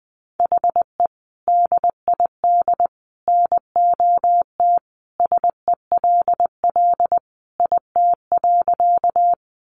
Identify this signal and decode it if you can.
Morse code, heard as a keyed tone, about 20 wpm, 700 Hz. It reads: HE DID NOT SELL IT.
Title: Sentences from Top 1100 Words 20wpm